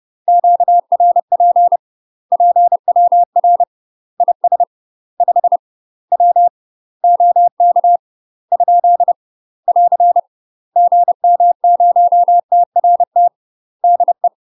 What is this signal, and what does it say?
Morse code, heard as a keyed tone, about 30 wpm, 700 Hz. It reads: QRP PWR IS 5 W OK ? <AR> GM0TRT DE
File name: UK MASTER QSOs TRAINING / 30WPM / UK MASTER QSOs 0035